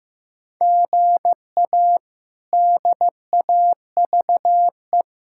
Morse code, heard as a keyed tone, about 15 wpm, 700 Hz. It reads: GA DAVE
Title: K1USN SST QSO Practice - 15wpm